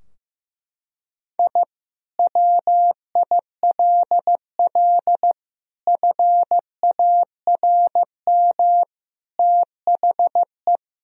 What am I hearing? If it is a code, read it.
Morse code, heard as a keyed tone, about 16 wpm, 700 Hz. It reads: I WILL FARM THE
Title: Sentences from Top 1000 Words - Review Part 2 - 15wpm